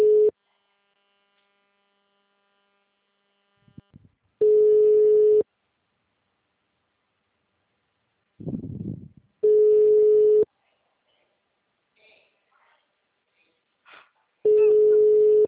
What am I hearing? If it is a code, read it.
Ringback tone (Europe, 425 Hz), with speech.
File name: out- -2002- - - .719